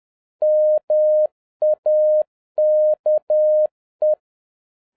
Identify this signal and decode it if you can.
Morse code, heard as a keyed tone, about 10 wpm, 610 Hz. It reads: MAKE